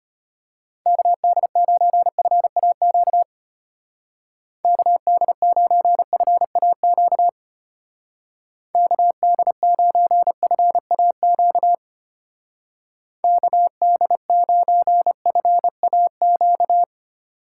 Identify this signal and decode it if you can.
Morse code, heard as a keyed tone, about 31 wpm, 700 Hz. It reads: KD9FAQ KD9FAQ KD9FAQ KD9FAQ
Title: Speed Racing - USA Call Signs 25wpm